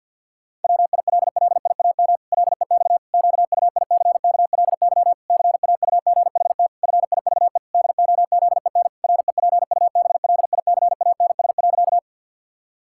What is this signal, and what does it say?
Morse code, heard as a keyed tone, about 49 wpm, 700 Hz. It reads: WILLIAM LEX QUICKLY CAUGHT FIVE DOZEN REPUBLICANS.
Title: Pangram Challenge v2 - 50wpm